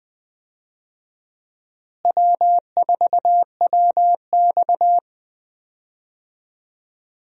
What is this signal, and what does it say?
Morse code, heard as a keyed tone, about 21 wpm, 700 Hz. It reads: W4WX
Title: USA Call Signs - 4 Characters - Repeated 3x - 20wpm